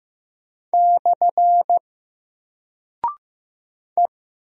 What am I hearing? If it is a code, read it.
Morse code, heard as a keyed tone, about 15 wpm, 700 Hz. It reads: / E
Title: Single Character (Letter:Number:Punctuation:Prosign) Alt 15wpm